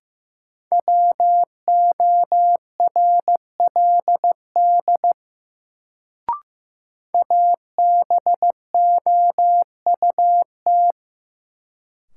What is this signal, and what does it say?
Morse code, heard as a keyed tone, about 15 wpm, 700 Hz. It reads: WORLD ABOUT
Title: Sets of 2 Words - New Word Drill - 15wpm